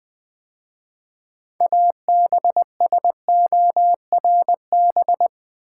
Morse code, heard as a keyed tone, about 20 wpm, 700 Hz. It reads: ABSORB
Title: 6 Letter Words - First 3 Letters Given - 20wpm